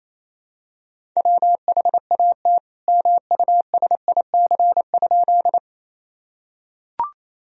Morse code, heard as a keyed tone, about 28 wpm, 700 Hz. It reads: WHAT MUSIC?
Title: Sentences from Top 1000 Words - Review Part 1 - 28wpm